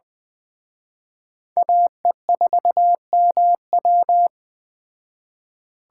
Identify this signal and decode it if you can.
Morse code, heard as a keyed tone, about 20 wpm, 700 Hz. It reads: AE4MW